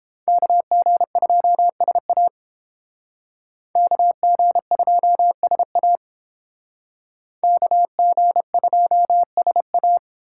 Morse code, heard as a keyed tone, about 30 wpm, 700 Hz. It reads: KG2SA KG2SA KG2SA